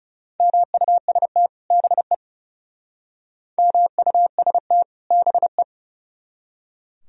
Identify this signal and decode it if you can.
Morse code, heard as a keyed tone, about 36 wpm, 700 Hz. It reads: MUST BE MUST BE
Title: Speed-Racing - Sets of 2 Words 30wpm